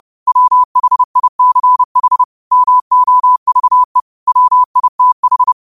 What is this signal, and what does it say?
Morse code, heard as a keyed tone, about 30 wpm, 1000 Hz. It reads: WHICH MOVE WITH
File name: qotdfiles / 30wpm